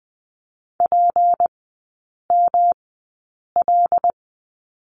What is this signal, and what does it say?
Morse code, heard as a keyed tone, about 20 wpm, spaced as for 8 wpm, 700 Hz. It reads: PML
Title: 152 Lesson 25e - Letter p - Three Characters 20wpm